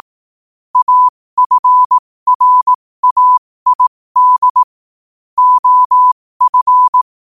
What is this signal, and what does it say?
Morse code, heard as a keyed tone, about 18 wpm, spaced as for 15 wpm, 1000 Hz. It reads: AFRAID OF
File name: qotdfiles / 15wpm